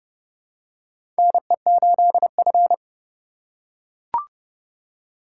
Morse code, heard as a keyed tone, about 30 wpm, 700 Hz. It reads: NE8F
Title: USA Call Signs - 4 Characters - 4x Character Spacing - 30wpm